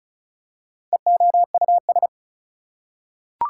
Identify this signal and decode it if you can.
Morse code, heard as a keyed tone, about 34 wpm, 700 Hz. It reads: EOUS
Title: Prefixes and Suffixes 35wpm